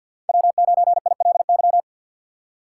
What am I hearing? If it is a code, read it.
Morse code, heard as a keyed tone, about 47 wpm, 700 Hz. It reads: W9ILY
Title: K1USN SST QSO Practice - 50wpm